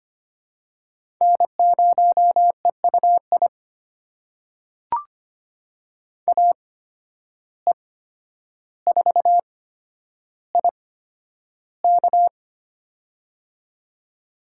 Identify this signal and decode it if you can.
Morse code, heard as a keyed tone, about 25 wpm, spaced as for 11 wpm, 700 Hz. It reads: N0EUI AE4IK
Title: USA Call Signs - 5 Characters - 8x Character Spacing - 25wpm